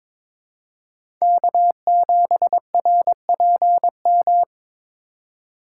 Morse code, heard as a keyed tone, about 22 wpm, 700 Hz. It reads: K7RPM
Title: USA Call Signs - 8x Character Spacing - 22wpm